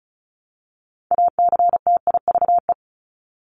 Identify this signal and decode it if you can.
Morse code, heard as a keyed tone, about 34 wpm, 700 Hz. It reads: ACTIVE
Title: Top 751-1000 Words 35wpm